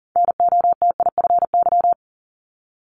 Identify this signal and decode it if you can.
Morse code, heard as a keyed tone, about 41 wpm, 690 Hz. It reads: NOTIFY
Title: Multi-letter Phonemes 40wpm